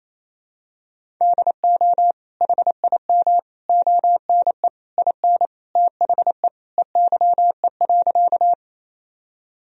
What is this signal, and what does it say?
Morse code, heard as a keyed tone, about 28 wpm, 700 Hz. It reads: DO HIM ONE IN THE EYE.
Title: Sentences from Top 1000 Words - Review Part 1 - 28wpm